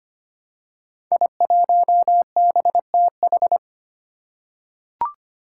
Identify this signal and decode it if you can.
Morse code, heard as a keyed tone, about 25 wpm, 700 Hz. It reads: I1BTH
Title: Five Challenging Letter-Number Combinations 25wpm